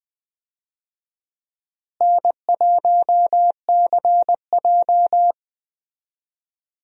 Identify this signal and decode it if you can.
Morse code, heard as a keyed tone, about 20 wpm, 700 Hz. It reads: N1CJ
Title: USA Call Signs - 4 Characters - Repeated 2x - 20wpm